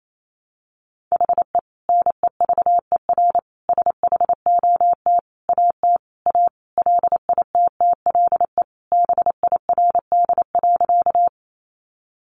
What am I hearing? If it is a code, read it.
Morse code, heard as a keyed tone, about 28 wpm, 700 Hz. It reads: HE NEVER SHOT AT A LITTLE BIRD.
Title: Sentences from Top 1100 Words 28wpm